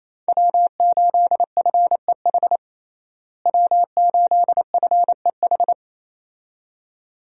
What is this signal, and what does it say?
Morse code, heard as a keyed tone, about 28 wpm, 700 Hz. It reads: W8FEH W8FEH
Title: USA Call Signs - 5 Characters - Repeated 2x - 28wpm